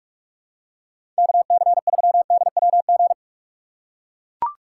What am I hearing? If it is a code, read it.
Morse code, heard as a keyed tone, about 42 wpm, 700 Hz. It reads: KC3DWG
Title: USA Call Signs - 4x Character Spacing - 45wpm